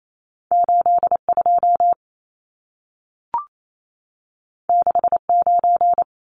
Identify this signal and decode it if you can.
Morse code, heard as a keyed tone, about 28 wpm, 700 Hz. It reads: 82 69
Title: Numbers - 2-Digit 28wpm